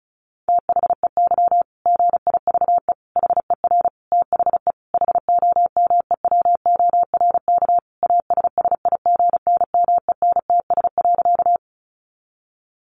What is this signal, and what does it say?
Morse code, heard as a keyed tone, about 34 wpm, 700 Hz. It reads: THEY GIVE HER THE HOMEWORK ASSIGNMENTS.
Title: Easy Phrases 35wpm